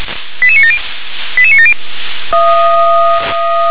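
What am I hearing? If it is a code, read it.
Five-tone selective call (ZVEI), heard as quick runs of five tones, tones of about 80 ms. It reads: 88970 88970
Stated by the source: ZVEI five-tone call received by SDR.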